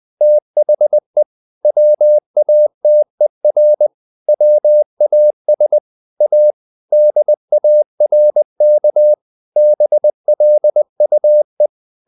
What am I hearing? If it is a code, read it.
Morse code, heard as a keyed tone, about 20 wpm, 600 Hz. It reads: THE WATER WAS A DARK BLUE
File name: the old man and the sea / the old man and the sea part 1 20wpm